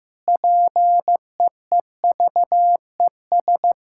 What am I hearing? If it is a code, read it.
Morse code, heard as a keyed tone, about 15 wpm, 700 Hz. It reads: PEEVES
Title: Pangram Challenge v2 - 15wpm